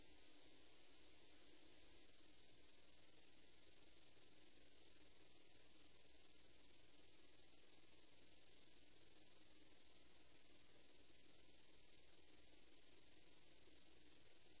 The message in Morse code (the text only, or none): none